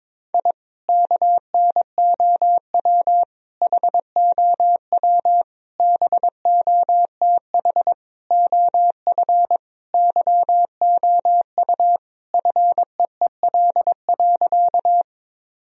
Morse code, heard as a keyed tone, about 22 wpm, 700 Hz. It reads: I KNOW HOW BOTH OF YOU FEEL.
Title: Sentences from Top 1000 Words - Review Part 1 - 22wpm